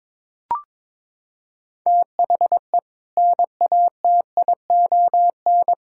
Morse code, heard as a keyed tone, about 21 wpm, 700 Hz. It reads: THE NATION
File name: practice / 22wpm / Sentences with 2-Word Phrases - 22wpm